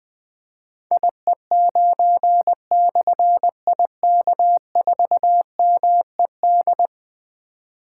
Morse code, heard as a keyed tone, about 21 wpm, 700 Hz. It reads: IE9/IK4MED
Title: Difficult Call Signs 20wpm